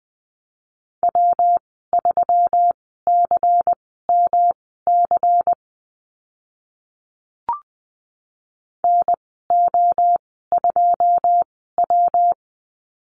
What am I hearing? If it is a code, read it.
Morse code, heard as a keyed tone, about 20 wpm, spaced as for 14 wpm, 700 Hz. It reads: W3CMC NO2W
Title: 128 Lesson 21g - Number 6 - Call Signs 20wpm